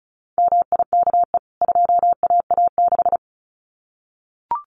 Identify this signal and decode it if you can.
Morse code, heard as a keyed tone, about 34 wpm, 700 Hz. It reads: MIKE 2AA6
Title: CWT Exchange 35wpm